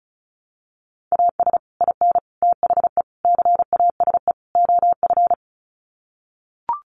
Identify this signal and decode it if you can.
Morse code, heard as a keyed tone, about 34 wpm, 700 Hz. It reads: AS IN THE CASE OF
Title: Sets of 5 Words 35wpm